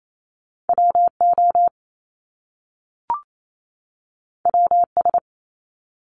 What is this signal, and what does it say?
Morse code, heard as a keyed tone, about 28 wpm, 700 Hz. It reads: W O W S
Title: Top Two Letter Combinations in Words 28wpm